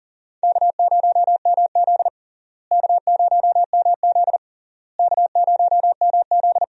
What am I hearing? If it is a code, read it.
Morse code, heard as a keyed tone, about 40 wpm, 700 Hz. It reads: K0MZ K0MZ K0MZ
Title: USA Call Signs - 4 Characters - Repeated 3x - 40wpm